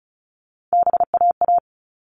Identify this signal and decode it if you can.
Morse code, heard as a keyed tone, about 34 wpm, 700 Hz. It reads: BAA
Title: All Cut Numbers - 3-Digit 35wpm